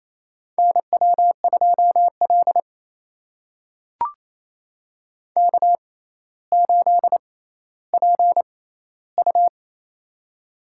Morse code, heard as a keyed tone, about 28 wpm, spaced as for 14 wpm, 700 Hz. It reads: NW2L K8PU
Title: USA Call Signs - 4 Characters - 6x Character Spacing - 28wpm